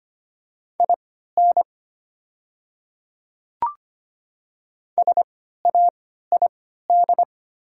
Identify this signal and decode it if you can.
Morse code, heard as a keyed tone, about 25 wpm, spaced as for 14 wpm, 700 Hz. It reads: IN SAID
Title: 3x Character Spacing - Top 100 Words - 25wpm